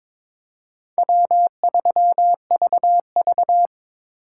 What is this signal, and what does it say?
Morse code, heard as a keyed tone, about 26 wpm, 700 Hz. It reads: W3VV
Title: Speed Racing - USA 1x2 Call Signs 22wpm